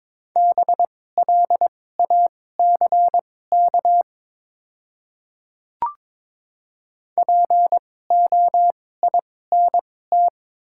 Morse code, heard as a keyed tone, about 22 wpm, spaced as for 16 wpm, 700 Hz. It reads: BLACK POINT
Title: 2x Character Spacing - Top 500 Words - 22wpm